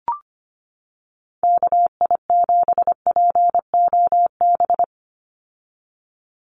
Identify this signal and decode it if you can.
Morse code, heard as a keyed tone, about 25 wpm, 700 Hz. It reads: KI7POB
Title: USA Call Signs 25wpm